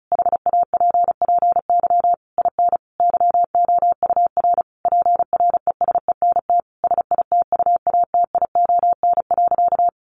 Morse code, heard as a keyed tone, about 34 wpm, 700 Hz. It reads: HAPPY IN YOUR PRESENT SITUATION.
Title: Sentences from Top 1200 Words 35wpm